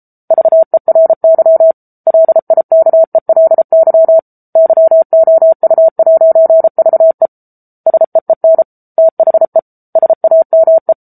Morse code, heard as a keyed tone, about 34 wpm, 650 Hz. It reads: VERY LIKELY YOU'VE SEEN THE SAME
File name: Book Two / 07-Chapter 7 - 33 WPM